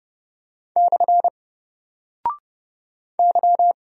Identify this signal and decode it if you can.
Morse code, heard as a keyed tone, about 30 wpm, 700 Hz. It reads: / Y
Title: Single Character (Letter:Number:Punctuation:Prosign) 30wpm